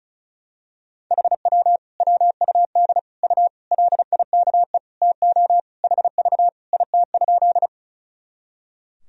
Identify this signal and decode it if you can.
Morse code, heard as a keyed tone, about 34 wpm, 700 Hz. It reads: HW WUD U LIKE TO HV IT?
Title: Sentences from Top 100 Words and Abbreviations 35wpm